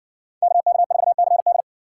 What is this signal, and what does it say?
Morse code, heard as a keyed tone, about 58 wpm, 700 Hz. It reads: KK4CD